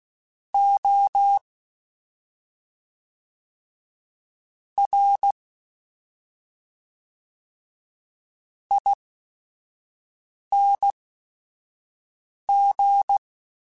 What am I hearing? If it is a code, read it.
Morse code, heard as a keyed tone, about 16 wpm, spaced as for 5 wpm, 800 Hz. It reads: O R ING
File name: Special INGs / Special INGs-mp3s / SlowToFaster ing @5